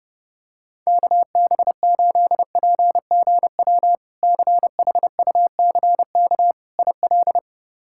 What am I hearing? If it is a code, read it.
Morse code, heard as a keyed tone, about 32 wpm, 700 Hz. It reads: KB8PGW CHUCK IL
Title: K1USN SST QSO Practice - 30wpm